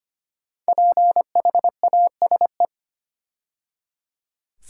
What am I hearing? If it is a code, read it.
Morse code, heard as a keyed tone, about 25 wpm, 700 Hz. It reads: PHASE
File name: 5 Letter Words - First 3 Letters Given - 25wpm